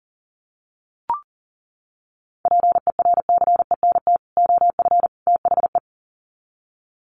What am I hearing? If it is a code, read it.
Morse code, heard as a keyed tone, about 40 wpm, 700 Hz. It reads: PERCENT OF THE
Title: Sets of 3 Words 40wpm